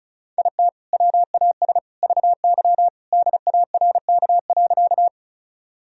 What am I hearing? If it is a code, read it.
Morse code, heard as a keyed tone, about 34 wpm, 700 Hz. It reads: IT WAS VY DARK.
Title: Sentences from Top 500 Words and Abbreviations 35wpm